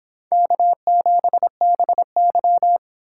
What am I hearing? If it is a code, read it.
Morse code, heard as a keyed tone, about 32 wpm, 700 Hz. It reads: K7BY